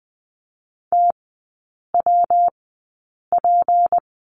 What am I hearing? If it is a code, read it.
Morse code, heard as a keyed tone, about 20 wpm, spaced as for 8 wpm, 700 Hz. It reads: TWP
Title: 172 Lesson 28e - Number 9 - Three Characters 20wpm